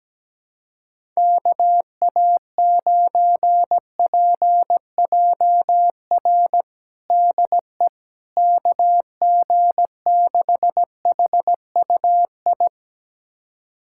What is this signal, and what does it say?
Morse code, heard as a keyed tone, about 17 wpm, 700 Hz. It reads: KA9PJR DE KG6HUI
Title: USA Callsign DE Callsign 17wpm